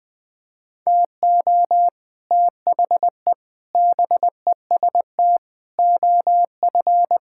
Morse code, heard as a keyed tone, about 21 wpm, 700 Hz. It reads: TO THE BEST OF
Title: Sets of 8 Words - 20wpm